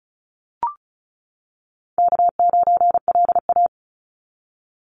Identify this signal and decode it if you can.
Morse code, heard as a keyed tone, about 36 wpm, 700 Hz. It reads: K9LA
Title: Call Signs - 35wpm